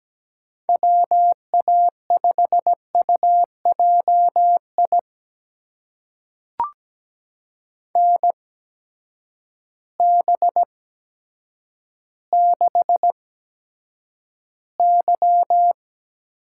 Morse code, heard as a keyed tone, about 17 wpm, spaced as for 8 wpm, 700 Hz. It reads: WA5UJI NB6Y
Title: USA Call Signs - 8x Character Spacing - 17wpm